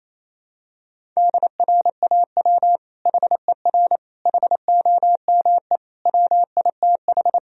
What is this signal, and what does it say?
Morse code, heard as a keyed tone, about 28 wpm, 700 Hz. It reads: DRAW HER HOME WITH
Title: Sentences from Top 1000 Words - Review Part 1 - 28wpm